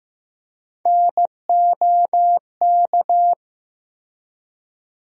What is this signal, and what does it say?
Morse code, heard as a keyed tone, about 15 wpm, 700 Hz. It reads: NOK
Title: USA Call Sign Postfixes - 3 Letters 15wpm